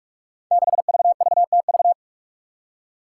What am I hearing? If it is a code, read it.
Morse code, heard as a keyed tone, about 42 wpm, 700 Hz. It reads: 6VVTV
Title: Five Challenging Letter-Number Combinations 45wpm